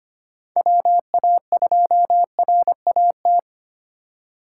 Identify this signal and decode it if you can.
Morse code, heard as a keyed tone, about 25 wpm, 700 Hz. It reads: WA2RAT
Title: USA Call Signs - 6 Characters - 3x Character Spacing - 25wpm